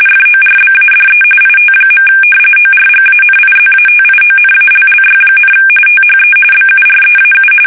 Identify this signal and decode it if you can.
RTTY, about 45.45 baud. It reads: FORBADE IT. ORGANIZATIONALLY, THE BUREAU
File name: fsk850